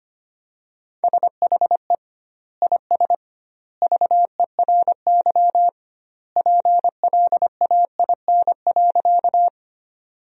Morse code, heard as a keyed tone, about 25 wpm, 700 Hz. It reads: SHE IS VERY PLAIN.
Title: 2x Word-Spacing - Sentences from Top 400 Words - 25wpm